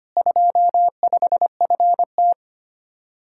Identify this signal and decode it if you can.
Morse code, heard as a keyed tone, about 30 wpm, 700 Hz. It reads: 25FT
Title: Speed-Racing Common QSO Phrases Encore - 25wpm